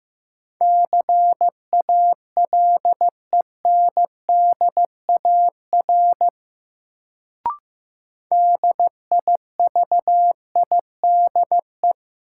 Morse code, heard as a keyed tone, about 15 wpm, 700 Hz. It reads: CALENDAR DIVIDE
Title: Sentences from Top 1600 Words - New Word Drill - 15wpm